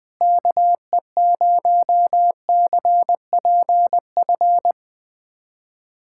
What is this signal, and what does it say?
Morse code, heard as a keyed tone, about 20 wpm, 700 Hz. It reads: KE0CPF
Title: USA Callsign DE Callsign - 6 Characters - 20wpm